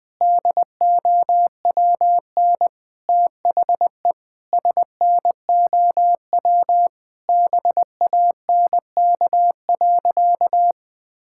Morse code, heard as a keyed tone, about 20 wpm, 700 Hz. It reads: DOWN THE SNOW BANK.
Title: Sentences from Top 1600 Words 20wpm